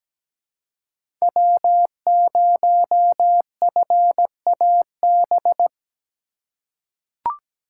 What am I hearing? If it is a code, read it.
Morse code, heard as a keyed tone, about 17 wpm, 700 Hz. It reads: W0FAB
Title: USA Call Signs - 5 Characters - 4x Character Spacing - 17wpm